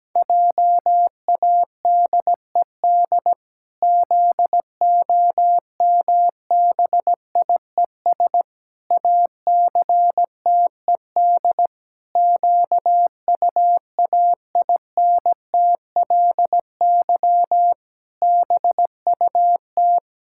Morse code, heard as a keyed tone, about 17 wpm, 700 Hz. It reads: JADED ZOMBIES ACTED QUAINTLY BUT
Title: Pangram Challenge v2 - 17wpm